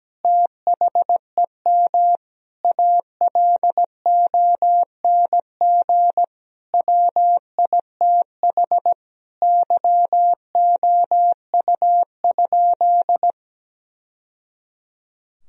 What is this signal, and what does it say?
Morse code, heard as a keyed tone, about 17 wpm, 700 Hz. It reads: THEM ALONG WITH YOU?
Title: Sentences from Top 1000 Words - Review Part 3 - 17wpm